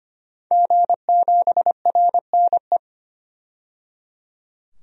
Morse code, heard as a keyed tone, about 25 wpm, 700 Hz. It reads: G7RNE
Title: DXCC-100 Call Signs - 5 Characters - 25wpm